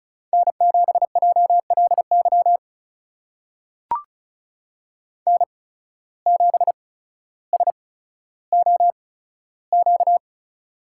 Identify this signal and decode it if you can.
Morse code, heard as a keyed tone, about 36 wpm, spaced as for 15 wpm, 700 Hz. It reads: N7JLY N7SOQ
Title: USA Call Signs - 5 Characters - 8x Character Spacing - 35wpm